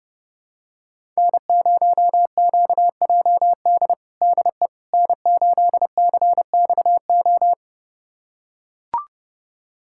Morse code, heard as a keyed tone, about 32 wpm, 700 Hz. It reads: N0QJD DE N8CXO
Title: USA Callsign DE Callsign - 5 Characters - 30wpm